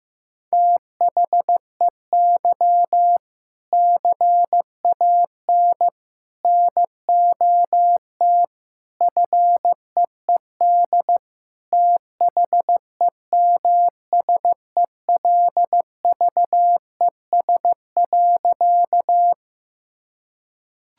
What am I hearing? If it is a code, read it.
Morse code, heard as a keyed tone, about 15 wpm, 700 Hz. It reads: THEY CAN NOT FEED THEMSELVES.
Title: Sentences from Top 1600 Words 15wpm